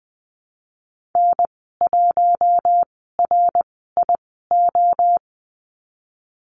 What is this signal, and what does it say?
Morse code, heard as a keyed tone, about 20 wpm, spaced as for 15 wpm, 700 Hz. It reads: N1RIO
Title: 082 Lesson 14g - Number 2 - Call Signs 20wpm